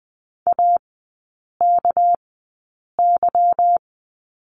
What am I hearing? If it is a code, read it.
Morse code, heard as a keyed tone, about 20 wpm, spaced as for 8 wpm, 700 Hz. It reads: AKY
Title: 198 Lesson 32e - Letter k - Three Characters 20wpm